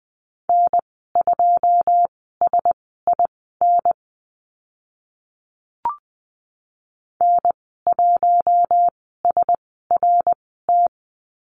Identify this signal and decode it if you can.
Morse code, heard as a keyed tone, about 20 wpm, spaced as for 15 wpm, 700 Hz. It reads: N2SIN N1SRT
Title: 082 Lesson 14g - Number 2 - Call Signs 20wpm